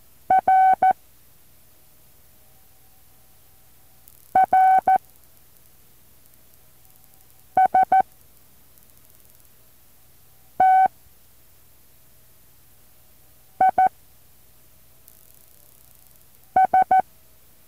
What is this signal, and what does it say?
Morse code, heard as a keyed tone, about 14 wpm, spaced as for 3 wpm, 760 Hz. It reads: R RST IS
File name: intl morse code2